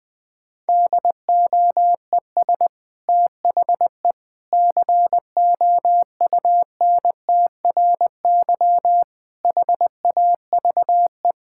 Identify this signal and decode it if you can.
Morse code, heard as a keyed tone, about 20 wpm, 700 Hz. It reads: DOES THE COUNTRY HAVE
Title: Sentences from Top 1000 Words - Review Part 5 - 20wpm